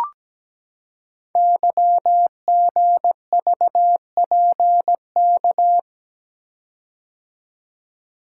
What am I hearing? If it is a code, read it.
Morse code, heard as a keyed tone, about 17 wpm, 700 Hz. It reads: YGVPK
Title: Code Groups - 5 Letters - 17wpm